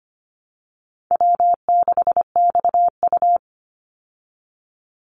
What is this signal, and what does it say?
Morse code, heard as a keyed tone, about 25 wpm, 700 Hz. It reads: W6XU
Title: Call Signs Repeated 3x - 25wpm